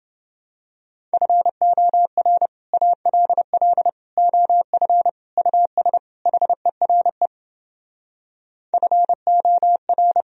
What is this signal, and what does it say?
Morse code, heard as a keyed tone, about 31 wpm, 700 Hz. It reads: FOR ALL OF US HERE FOR